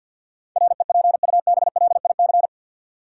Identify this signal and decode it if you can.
Morse code, heard as a keyed tone, about 52 wpm, 700 Hz. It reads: REPUBLIC
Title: Sentences from Top 1500 Words - New Word Drill Alt - 50wpm